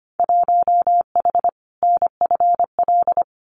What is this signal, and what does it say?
Morse code, heard as a keyed tone, about 25 wpm, 700 Hz. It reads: 1H NFL
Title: Winter Field Day Exchanges 25wpm